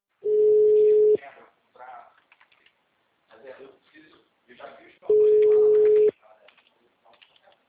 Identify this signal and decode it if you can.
Ringback tone (Europe, 425 Hz), with speech.